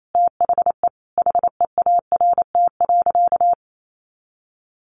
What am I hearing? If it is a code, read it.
Morse code, heard as a keyed tone, about 28 wpm, 700 Hz. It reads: THE HEART.
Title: Sentences from Top 1200 Words 28wpm